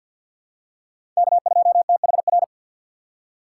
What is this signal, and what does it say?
Morse code, heard as a keyed tone, about 47 wpm, 700 Hz. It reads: K2THR
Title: USA Call Signs - 5 Characters - 3x Character Spacing - 50wpm